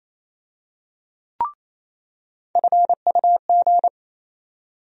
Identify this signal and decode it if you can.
Morse code, heard as a keyed tone, about 28 wpm, 700 Hz. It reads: FUG
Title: USA Call Sign Postfixes - 3 Letters 28wpm